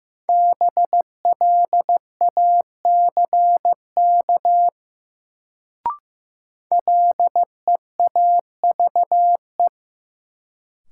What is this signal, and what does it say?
Morse code, heard as a keyed tone, about 15 wpm, 700 Hz. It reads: BLACK LEAVE
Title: Sentences from Top 400 Words - New Word Drill - 15wpm